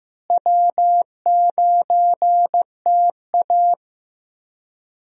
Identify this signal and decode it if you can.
Morse code, heard as a keyed tone, about 16 wpm, 700 Hz. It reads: W9TA
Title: USA Call Signs - 4 Characters - Repeated 3x - 15wpm